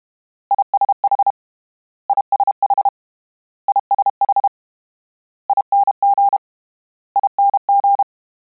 Morse code, heard as a keyed tone, about 32 wpm, 800 Hz. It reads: ISH ISH ISH ING ING
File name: Combinations / Combinations-mp3s / Combinations@30